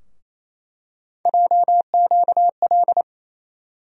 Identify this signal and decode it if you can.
Morse code, heard as a keyed tone, about 28 wpm, 700 Hz. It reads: JQL
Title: USA Call Sign Postfixes - 3 Letters 28wpm